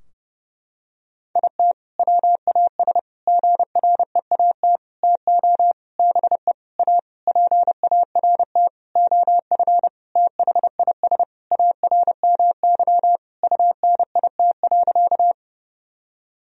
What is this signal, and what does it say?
Morse code, heard as a keyed tone, about 30 wpm, 700 Hz. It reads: IT WAS GREAT TO BE A PART OF THIS ARMY UNIT.
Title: Sentences from Top 1000 Words - Review Part 2 - 30wpm